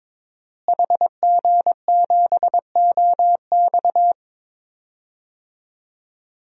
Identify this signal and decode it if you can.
Morse code, heard as a keyed tone, about 22 wpm, 700 Hz. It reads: HG7OX
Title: DXCC-100 Call Signs - 5 Characters - Repeated 2x - 22wpm